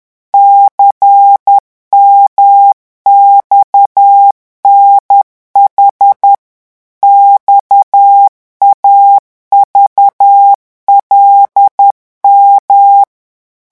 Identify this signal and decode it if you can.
Morse code, heard as a keyed tone, about 10 wpm, 800 Hz. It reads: CMXNH XAVLM
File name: morse mp3 / revision5-vitesse 10 mots